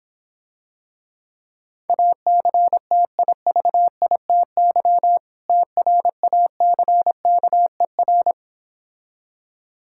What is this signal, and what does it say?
Morse code, heard as a keyed tone, about 26 wpm, 690 Hz. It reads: ACTIVITY TRACKER